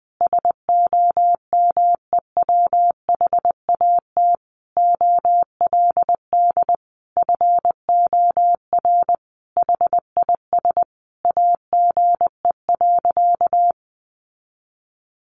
Morse code, heard as a keyed tone, about 20 wpm, 700 Hz. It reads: SOMEWHAT OLD FOR HIS AGE.
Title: Sentences from Top 1300 Words 20wpm